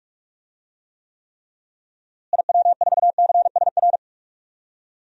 Morse code, heard as a keyed tone, about 46 wpm, 700 Hz. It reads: IW4CSR
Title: DXCC-100 Call Signs - 6 Characters - Repeated 3x - 45wpm